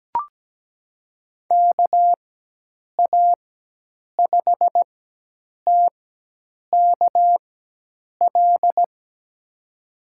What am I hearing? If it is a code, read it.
Morse code, heard as a keyed tone, about 17 wpm, spaced as for 8 wpm, 700 Hz. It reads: KA5TKL
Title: USA Call Signs - 4x Character Spacing - 17wpm